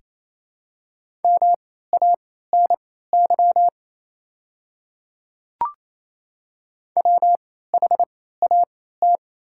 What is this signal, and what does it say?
Morse code, heard as a keyed tone, about 28 wpm, spaced as for 16 wpm, 690 Hz. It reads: MANY WHAT
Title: 3x Character Spacing - Top 100 Words - 28wpm